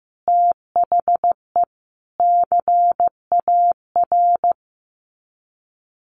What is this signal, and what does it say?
Morse code, heard as a keyed tone, about 15 wpm, 700 Hz. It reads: THE CAR
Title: Sets of 5 Words Encore 15wpm